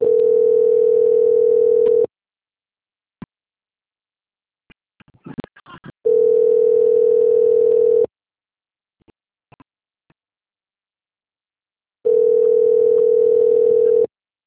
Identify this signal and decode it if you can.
Ringback tone (North America), 440 + 480 Hz.